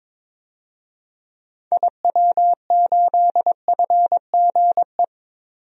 Morse code, heard as a keyed tone, about 22 wpm, 700 Hz. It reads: IW8FGE
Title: DXCC-100 Call Signs - 6 Characters - 22wpm